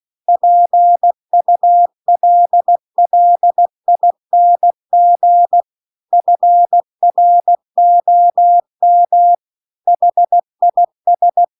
Morse code, heard as a keyed tone, about 16 wpm, 700 Hz. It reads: PULLING FROM HIS